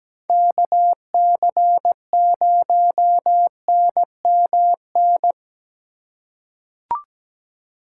Morse code, heard as a keyed tone, about 17 wpm, 700 Hz. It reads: KC0NMN
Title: USA Call Signs - 6 Characters - 2x Character Spacing - 17wpm